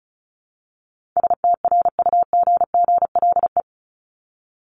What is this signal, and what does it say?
Morse code, heard as a keyed tone, about 34 wpm, 700 Hz. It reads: STRUGGLE
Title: Top 1751-2000 Words 35wpm